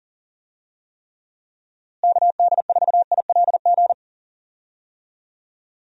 Morse code, heard as a keyed tone, about 37 wpm, 700 Hz. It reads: KD4ILG
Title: USA Call Signs - 6 Characters - Repeated 2x - 40wpm